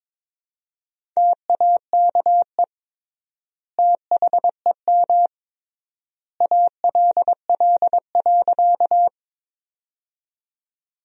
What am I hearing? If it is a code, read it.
Morse code, heard as a keyed tone, about 21 wpm, 700 Hz. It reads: TAKE THEM ALL.
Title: 3x Word-Spacing - Sentences from Top 200 Words - 22wpm